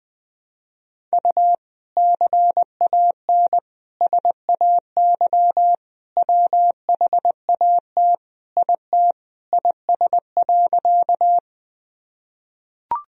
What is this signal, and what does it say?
Morse code, heard as a keyed tone, about 20 wpm, 700 Hz. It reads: U CAN SAY WHAT IT IS.
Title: Sentences from Top 200 Words and Abbreviations 20wpm